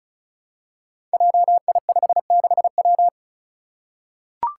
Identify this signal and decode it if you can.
Morse code, heard as a keyed tone, about 36 wpm, 700 Hz. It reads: JI56W
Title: Five Challenging Letter-Number Combinations 35wpm